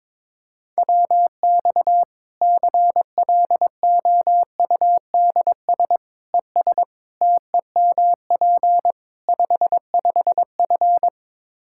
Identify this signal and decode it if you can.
Morse code, heard as a keyed tone, about 21 wpm, 700 Hz. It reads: WX CLOUDS ES TEMP 55F
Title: Practice QSO - Exchange 2 - Info - 22wpm